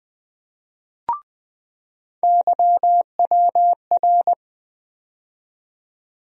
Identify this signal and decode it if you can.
Morse code, heard as a keyed tone, about 20 wpm, 700 Hz. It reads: YWR
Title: USA Call Sign Postfixes - 3 Letters 20wpm